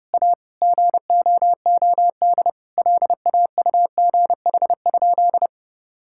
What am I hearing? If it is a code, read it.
Morse code, heard as a keyed tone, about 30 wpm, 700 Hz. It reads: A GOOD LAUGH?
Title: Sentences from Top 1000 Words - Review Part 5 - 30wpm